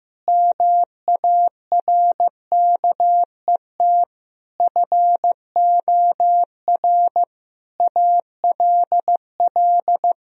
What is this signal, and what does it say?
Morse code, heard as a keyed tone, about 15 wpm, 700 Hz. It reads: MARKET FOR ALL
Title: Sentences from Top 1000 Words - Review Part 3 - 15wpm